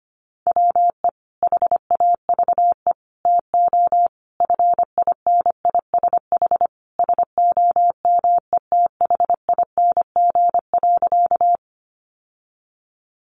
Morse code, heard as a keyed tone, about 25 wpm, 700 Hz. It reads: WE HAVE TO FINISH SOMETHING.
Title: Sentences from Top 1200 Words 25wpm